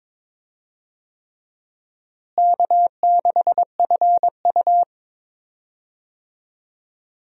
Morse code, heard as a keyed tone, about 22 wpm, 700 Hz. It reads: K6FU
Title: USA Call Signs - 4 Characters - Repeated 3x - 22wpm